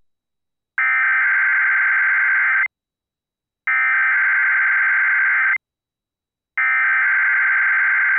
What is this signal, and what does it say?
Emergency Alert System (SAME) tones (header bursts), heard as harsh buzzing bursts. It is heard from 0.8 to 7.1 s.